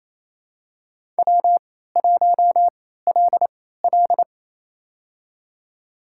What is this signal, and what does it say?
Morse code, heard as a keyed tone, about 28 wpm, spaced as for 16 wpm, 700 Hz. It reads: W1LL
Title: USA Call Signs - 4 Characters - 3x Character Spacing - 28wpm